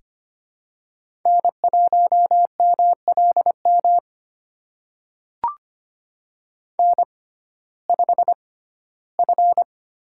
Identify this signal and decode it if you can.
Morse code, heard as a keyed tone, about 25 wpm, spaced as for 15 wpm, 700 Hz. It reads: N1MLM N5F
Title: USA Call Signs - 6x Character Spacing - 25wpm